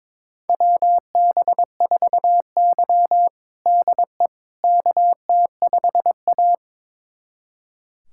Morse code, heard as a keyed tone, about 22 wpm, 700 Hz. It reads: WB4Y DE KT5A
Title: USA Callsign DE Callsign - 4 Characters - 22wpm